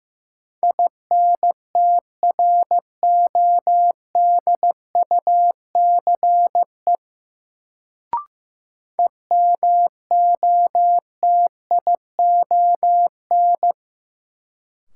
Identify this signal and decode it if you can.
Morse code, heard as a keyed tone, about 16 wpm, 700 Hz. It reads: INTRODUCE EMOTION
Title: Sentences from Top 1600 Words - New Word Drill - 15wpm